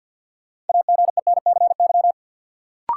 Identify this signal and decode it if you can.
Morse code, heard as a keyed tone, about 53 wpm, 700 Hz. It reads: AGENCY
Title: Sentences from Top 1300 Words - New Word Drill Alt - 50wpm